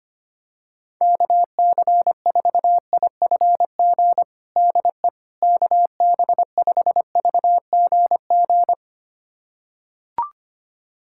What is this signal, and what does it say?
Morse code, heard as a keyed tone, about 25 wpm, 700 Hz. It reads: KC4IFG DE KB5VGG
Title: USA Callsign DE Callsign - 6 Characters - 25wpm